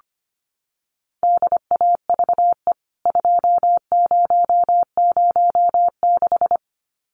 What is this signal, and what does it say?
Morse code, heard as a keyed tone, about 25 wpm, 700 Hz. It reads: DAVE 2006
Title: CWT Exchange 25wpm